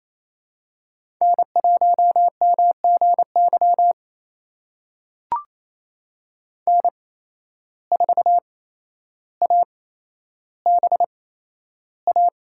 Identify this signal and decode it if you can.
Morse code, heard as a keyed tone, about 28 wpm, spaced as for 12 wpm, 700 Hz. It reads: N1MGY N4ABA
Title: USA Call Signs - 5 Characters - 8x Character Spacing - 28wpm